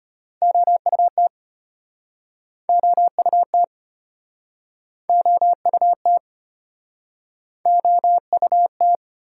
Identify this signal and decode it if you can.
Morse code, heard as a keyed tone, about 34 wpm, 700 Hz. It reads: OUT OUT OUT OUT